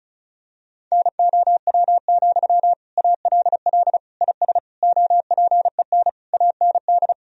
Morse code, heard as a keyed tone, about 34 wpm, 700 Hz. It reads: NOW, ALL IS OPEN AND
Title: Sentences from Top 1000 Words - Review Part 4 - 35wpm